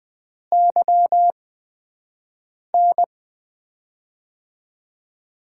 Morse code, heard as a keyed tone, about 20 wpm, spaced as for 5 wpm, 700 Hz. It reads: YN
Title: USA Call Signs - 6 Characters - 8x Character Spacing - 20wpm